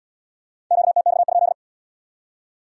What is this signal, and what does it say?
Morse code, heard as a keyed tone, about 71 wpm, 700 Hz. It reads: QTX?